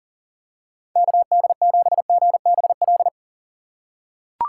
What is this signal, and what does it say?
Morse code, heard as a keyed tone, about 40 wpm, 700 Hz. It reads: KD7GBL
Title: USA Call Signs - 6 Characters - 2x Character Spacing - 40wpm